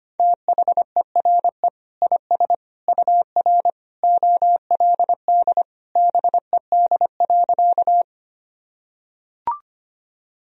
Morse code, heard as a keyed tone, about 25 wpm, 700 Hz. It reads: THERE IS UR OLD BED.
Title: Sentences from Top 500 Words and Abbreviations 25wpm